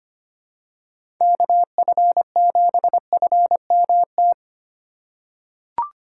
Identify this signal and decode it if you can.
Morse code, heard as a keyed tone, about 25 wpm, 700 Hz. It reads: KF7FMT
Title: USA Call Signs - 6 Characters - 2x Character Spacing - 25wpm